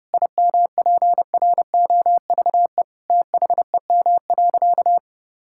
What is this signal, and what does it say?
Morse code, heard as a keyed tone, about 30 wpm, 700 Hz. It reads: IMPROVE THEM.
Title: Sentences from Top 1000 Words - Review Part 5 - 30wpm